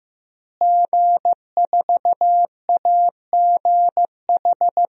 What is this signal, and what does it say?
Morse code, heard as a keyed tone, about 15 wpm, 700 Hz. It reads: G4AGH
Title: DXCC-100 Call Signs - 5 Characters - Repeated 3x - 15wpm